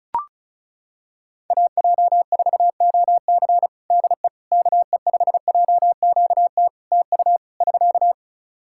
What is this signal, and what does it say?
Morse code, heard as a keyed tone, about 34 wpm, 700 Hz. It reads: AJ4OC DE KE5JQT TU <SK>
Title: K1USN SST QSO Practice 35wpm